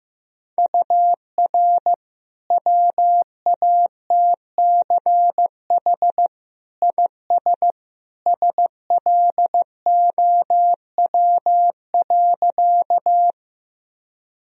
Morse code, heard as a keyed tone, about 15 wpm, 700 Hz. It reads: UR WATCH IS SLOW.
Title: Sentences from Top 400 Words and Abbreviations 15wpm